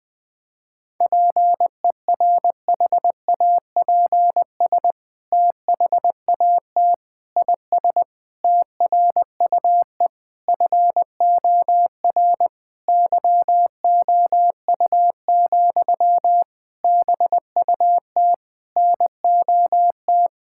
Morse code, heard as a keyed tone, about 20 wpm, 700 Hz. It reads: PERHAPS THAT IS TRUE FOR YOU, BUT NOT
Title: Sentences from Top 1000 Words - Review Part 5 - 20wpm